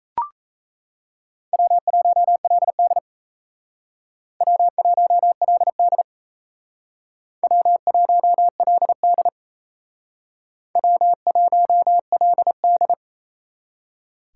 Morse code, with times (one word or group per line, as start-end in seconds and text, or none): 1.5-3.0 W1LD
4.4-6.0 W1LD
7.4-9.3 W1LD
10.8-12.9 W1LD